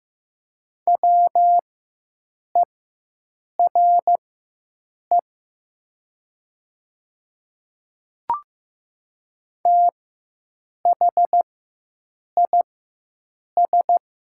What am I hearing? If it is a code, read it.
Morse code, heard as a keyed tone, about 15 wpm, spaced as for 7 wpm, 700 Hz. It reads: WERE THIS
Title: 4x Character Spacing - Top 100 Words - 15wpm